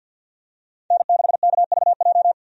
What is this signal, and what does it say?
Morse code, heard as a keyed tone, about 52 wpm, 700 Hz. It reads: N6KVJ
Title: USA Call Signs - 5 Characters - 50wpm